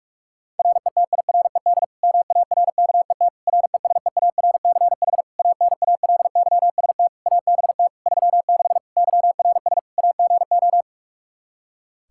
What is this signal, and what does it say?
Morse code, heard as a keyed tone, about 46 wpm, 700 Hz. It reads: RETIRED MARKET RESEARCH ANALYST ABT 36 YRS AGO
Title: Practice QSO - Exchange 3 - More Info - 45wpm